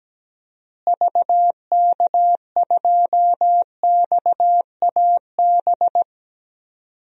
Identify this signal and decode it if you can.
Morse code, heard as a keyed tone, about 17 wpm, 700 Hz. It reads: VK2XAB
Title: DXCC-100 Call Signs - 6 Characters - Repeated 2x - 17wpm